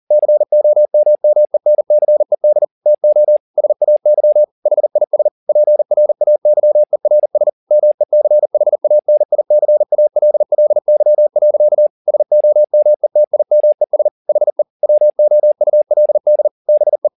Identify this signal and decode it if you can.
Morse code, heard as a keyed tone, about 40 wpm, 600 Hz. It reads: COMMENCED TO SAY HIS PRAYERS MECHANICALLY. SOMETIMES HE WOULD BE